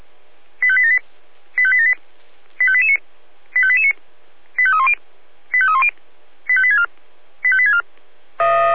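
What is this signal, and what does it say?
Five-tone selective call (ZVEI), heard as quick runs of five tones, tones of about 70 ms. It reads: 85878 85878 85889 85889 85310 85310 85864 85864
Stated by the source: Five-tone ZVEI selective call received by SDR.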